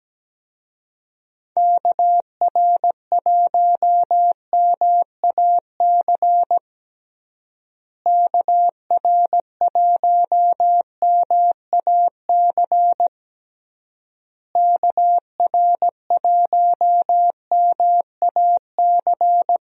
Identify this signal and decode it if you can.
Morse code, heard as a keyed tone, about 17 wpm, 700 Hz. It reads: KR1MAC KR1MAC KR1MAC
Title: USA Call Signs - 6 Characters - Repeated 3x - 17wpm